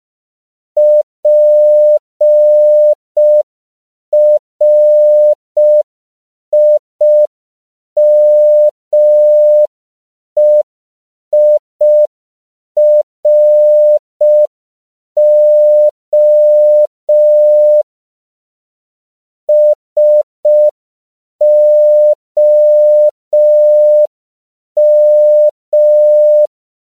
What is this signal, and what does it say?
Morse code, heard as a keyed tone, about 5 wpm, 600 Hz. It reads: PRIMEIRO SOM
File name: CW / 5PPM